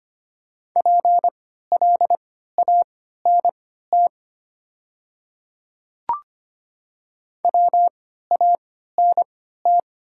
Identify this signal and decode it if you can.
Morse code, heard as a keyed tone, about 25 wpm, spaced as for 14 wpm, 700 Hz. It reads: PLANT WANT
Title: 3x Character Spacing - Top 500 Words - 25wpm